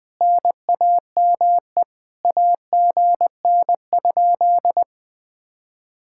Morse code, heard as a keyed tone, about 20 wpm, 700 Hz. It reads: NAME AGN?